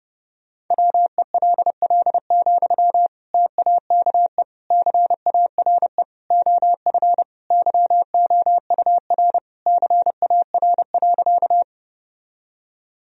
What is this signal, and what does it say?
Morse code, heard as a keyed tone, about 32 wpm, 700 Hz. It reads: WELL, TAKE CARE OF YOUR CAR.
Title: Sentences from Top 1000 Words - Review Part 1 - 30wpm